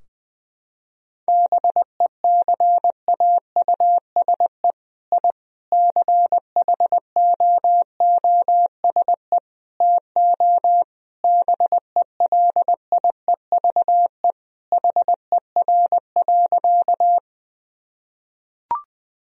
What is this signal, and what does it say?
Morse code, heard as a keyed tone, about 21 wpm, 700 Hz. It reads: BECAUSE I CHOOSE TO BELIEVE HER.
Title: Sentences from Top 1000 Words - Review Part 4 - 20wpm